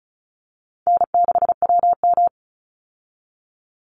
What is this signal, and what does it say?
Morse code, heard as a keyed tone, about 34 wpm, 700 Hz. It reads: N6WM
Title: Call Signs Repeated 2x - 35wpm